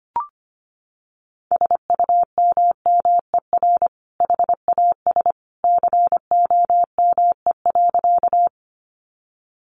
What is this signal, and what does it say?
Morse code, heard as a keyed tone, about 25 wpm, 700 Hz. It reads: SUMMER HAS COME.
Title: Sentences from Top 1100 Words 25wpm